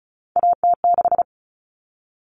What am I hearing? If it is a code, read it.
Morse code, heard as a keyed tone, about 34 wpm, 700 Hz. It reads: AT6
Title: All Cut Numbers - 3-Digit 35wpm